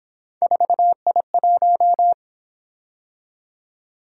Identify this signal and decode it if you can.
Morse code, heard as a keyed tone, about 28 wpm, 700 Hz. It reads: 4I1